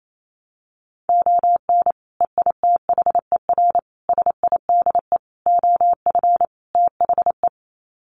Morse code, heard as a keyed tone, about 28 wpm, 700 Hz. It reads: ON EITHER SIDE OF THE
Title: Sets of 5 Words 28wpm